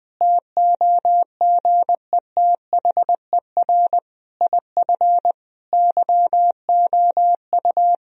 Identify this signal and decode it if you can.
Morse code, heard as a keyed tone, about 20 wpm, 700 Hz. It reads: TOGETHER IF YOU
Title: Sentences from Top 1000 Words - Review Part 2 - 20wpm